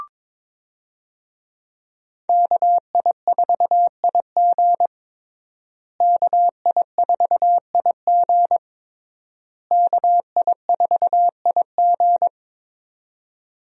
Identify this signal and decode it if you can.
Morse code, heard as a keyed tone, about 22 wpm, 700 Hz. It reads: KI4IG KI4IG KI4IG
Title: USA Call Signs - 5 Characters - Repeated 3x - 22wpm